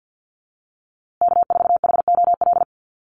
Morse code, heard as a keyed tone, about 50 wpm, 700 Hz. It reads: K4HOL
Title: USA Call Signs Repeated 3x 50wpm